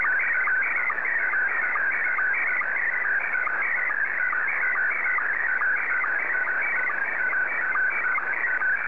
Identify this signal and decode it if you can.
SSTV (slow-scan television), Scottie 1.
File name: sstv / S1-JA3QJQ